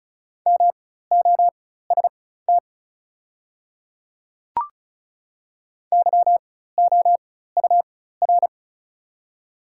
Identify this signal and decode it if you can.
Morse code, heard as a keyed tone, about 36 wpm, spaced as for 16 wpm, 700 Hz. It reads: MOST YOUR
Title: 4x Character Spacing - Top 100 Words - 35wpm